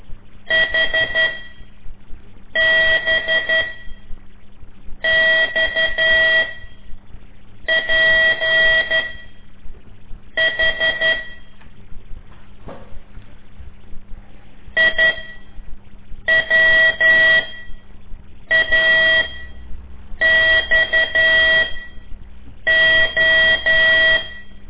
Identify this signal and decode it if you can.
Morse code, heard as a keyed tone, about 12 wpm, spaced as for 6 wpm, 1950 Hz. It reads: HBXPH IWAXO
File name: ik1weg2 / CORSI CW / lez5-1